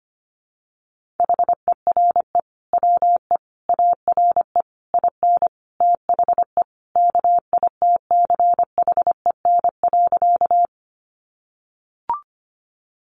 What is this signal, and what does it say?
Morse code, heard as a keyed tone, about 25 wpm, 700 Hz. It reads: HERE WE ARE IN THE KITCHEN.
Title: Sentences from Top 1400 Words 25wpm